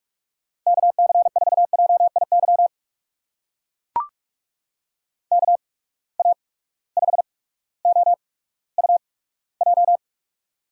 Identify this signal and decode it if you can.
Morse code, heard as a keyed tone, about 45 wpm, spaced as for 19 wpm, 700 Hz. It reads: KC4JIY KA5OUJ
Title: USA Call Signs - 6 Characters - 8x Character Spacing - 45wpm